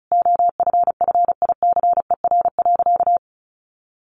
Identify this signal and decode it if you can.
Morse code, heard as a keyed tone, about 34 wpm, 700 Hz. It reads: OFFICER.
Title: Sentences from Top 1200 Words 35wpm